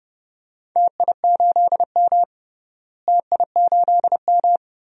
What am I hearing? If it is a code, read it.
Morse code, heard as a keyed tone, about 30 wpm, 700 Hz. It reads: TI8M TI8M
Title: DXCC-100 Call Signs - 4 Characters - Repeated 2x - 30wpm